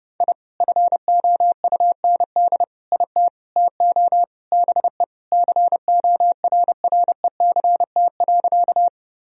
Morse code, heard as a keyed tone, about 30 wpm, 700 Hz. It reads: I FOUND IT TO BE CORRECT.
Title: Sentences from Top 1000 Words - Review Part 2 - 30wpm